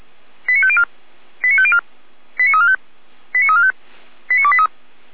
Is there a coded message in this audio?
Yes: five-tone selective call (ZVEI) from 0.5 to 4.7 s, heard as quick runs of five tones, tones of about 70 ms. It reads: 89504 89504 89345 89345 89283